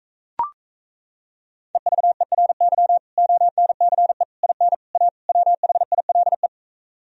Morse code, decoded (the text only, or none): EVERY ONCE IN A WHILE